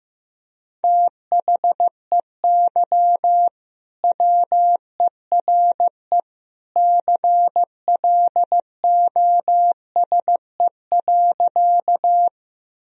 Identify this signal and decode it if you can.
Morse code, heard as a keyed tone, about 15 wpm, 700 Hz. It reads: THEY WERE CLOSE.
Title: Sentences from Top 1000 Words - Review Part 1 - 15wpm